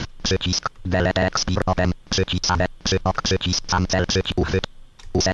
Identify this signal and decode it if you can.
Speech. There is no code.